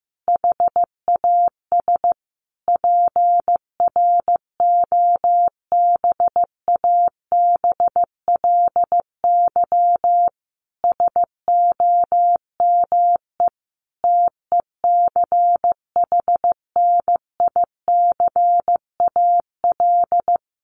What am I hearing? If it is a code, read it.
Morse code, heard as a keyed tone, about 15 wpm, 700 Hz. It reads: HAS PROBABLY SOME TECHNICAL
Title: Sentences from Top 1100 Words 15wpm